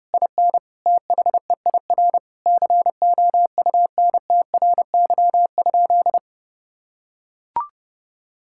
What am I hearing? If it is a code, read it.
Morse code, heard as a keyed tone, about 30 wpm, 700 Hz. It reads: IN THEIR COUNTRY?
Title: Sentences from Top 1000 Words - Review Part 5 - 30wpm